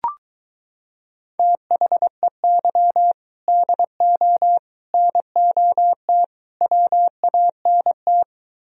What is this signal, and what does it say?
Morse code, heard as a keyed tone, about 23 wpm, 700 Hz. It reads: THEY DO NOT WANT